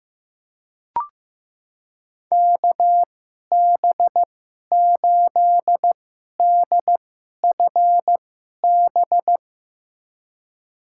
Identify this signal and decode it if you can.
Morse code, heard as a keyed tone, about 15 wpm, spaced as for 11 wpm, 700 Hz. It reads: KB8DFB
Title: USA Call Signs - 2x Character Spacing - 15wpm